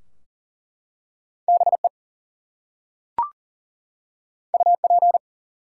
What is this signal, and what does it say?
Morse code, heard as a keyed tone, about 40 wpm, 700 Hz. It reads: BE UP
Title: Two Letter Words - 40wpm